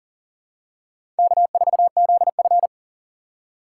Morse code, heard as a keyed tone, about 37 wpm, 700 Hz. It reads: K4ZF
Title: USA Call Signs - 4 Characters - 2x Character Spacing - 40wpm